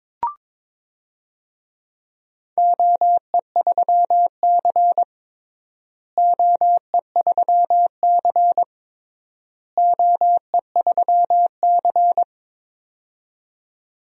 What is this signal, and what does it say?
Morse code, heard as a keyed tone, about 22 wpm, 700 Hz. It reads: OE3C OE3C OE3C
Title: DXCC-100 Call Signs - 4 Characters - Repeated 3x - 22wpm